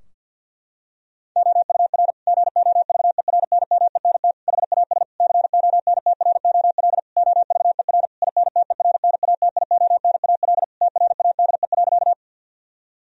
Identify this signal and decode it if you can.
Morse code, heard as a keyed tone, about 47 wpm, 700 Hz. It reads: OUR GOVERNMENT HAS CONTROL OVER INTERNATIONAL TRADE.
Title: Sentences from Top 1000 Words - Review Part 4 - 50wpm